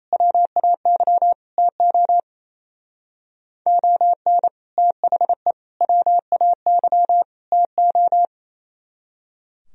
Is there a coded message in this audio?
Yes: Morse code from 0.1 to 8.2 s, heard as a keyed tone, about 36 wpm, 700 Hz. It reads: WAY TO ON THE WAY TO